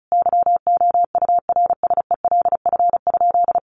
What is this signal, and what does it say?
Morse code, heard as a keyed tone, about 34 wpm, 700 Hz. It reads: YOURSELF?
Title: Sentences from Top 1200 Words 35wpm